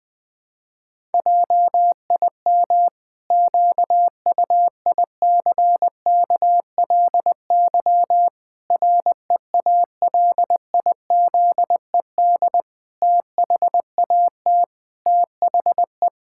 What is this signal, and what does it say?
Morse code, heard as a keyed tone, about 20 wpm, 700 Hz. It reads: JIM QUICKLY REALIZED THAT THE
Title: Pangram Challenge v2 - 20wpm